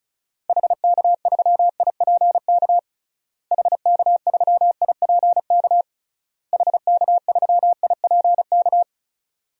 Morse code, heard as a keyed tone, about 36 wpm, 700 Hz. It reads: HK3IPK HK3IPK HK3IPK
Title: DXCC-100 Call Signs - 6 Characters - Repeated 3x - 35wpm